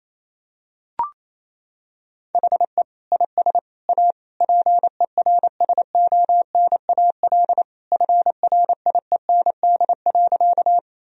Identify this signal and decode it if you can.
Morse code, heard as a keyed tone, about 28 wpm, 700 Hz. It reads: HE IS A PERSONAL FRIEND.
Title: Sentences from Top 1000 Words - Review Part 4 - 28wpm